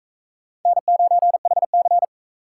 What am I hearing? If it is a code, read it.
Morse code, heard as a keyed tone, about 40 wpm, 700 Hz. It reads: N9HC